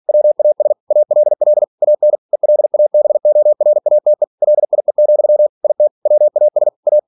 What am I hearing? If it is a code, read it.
Morse code, heard as a keyed tone, about 46 wpm, 600 Hz. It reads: WAS ALL AN ELABORATE LIE, IT WAS A
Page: Never 47wpm